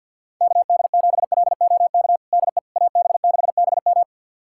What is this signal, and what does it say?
Morse code, heard as a keyed tone, about 52 wpm, 700 Hz. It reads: KD7LOK DE AB6BM
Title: USA Callsign DE Callsign 50wpm